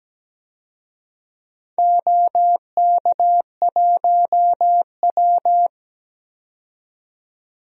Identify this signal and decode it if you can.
Morse code, heard as a keyed tone, about 17 wpm, 700 Hz. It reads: OK1W
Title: DXCC-100 Call Signs - 4 Characters - 17wpm